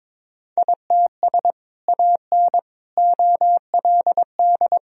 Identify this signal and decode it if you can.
Morse code, heard as a keyed tone, about 22 wpm, 700 Hz. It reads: ITS AN OLD
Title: Sentences from Top 1000 Words - Review Part 2 - 22wpm